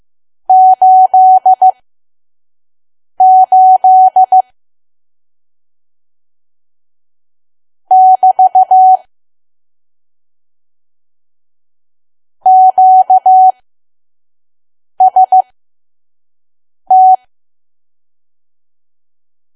Morse code, heard as a keyed tone, about 15 wpm, spaced as for 5 wpm, 750 Hz. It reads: 88 <BT> QST
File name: Archive / 5 WPM files / PM